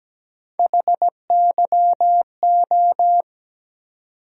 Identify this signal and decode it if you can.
Morse code, heard as a keyed tone, about 17 wpm, 700 Hz. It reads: HYO
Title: USA Call Sign Postfixes - 3 Letters 17wpm